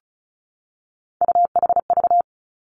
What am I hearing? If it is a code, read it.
Morse code, heard as a keyed tone, about 36 wpm, 700 Hz. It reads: UHV
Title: Three Challenging Letter-Number Combinations 35wpm